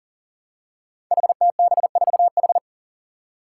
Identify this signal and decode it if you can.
Morse code, heard as a keyed tone, about 38 wpm, 700 Hz. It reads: HTB4H
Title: Five Challenging Letter-Number Combinations 40wpm